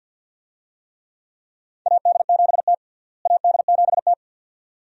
Morse code, heard as a keyed tone, about 48 wpm, 700 Hz. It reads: AD7T AD7T
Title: USA Call Signs - 4 Characters - Repeated 2x - 50wpm